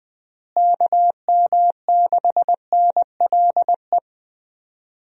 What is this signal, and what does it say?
Morse code, heard as a keyed tone, about 20 wpm, 700 Hz. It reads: KM6NLE
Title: USA Call Signs - 6 Characters - Repeated 3x - 20wpm